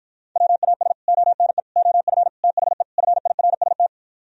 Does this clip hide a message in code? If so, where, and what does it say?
Yes: Morse code from 0.4 to 3.9 s, heard as a keyed tone, about 52 wpm, 700 Hz. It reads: WAS ONE OF THE FIRST